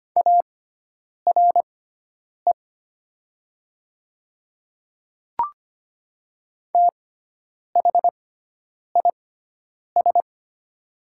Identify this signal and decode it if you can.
Morse code, heard as a keyed tone, about 25 wpm, spaced as for 9 wpm, 700 Hz. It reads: ARE THIS
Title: 6x Character Spacing - Top 300 Words - 25wpm